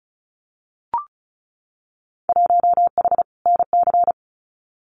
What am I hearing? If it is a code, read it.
Morse code, heard as a keyed tone, about 34 wpm, 700 Hz. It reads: 1H NC
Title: Winter Field Day Exchanges 35wpm